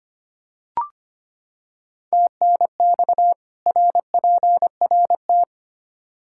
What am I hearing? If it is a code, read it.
Morse code, heard as a keyed tone, about 25 wpm, 700 Hz. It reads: TNX RPRT
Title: Practice QSO - Exchange 2 - Info - 25wpm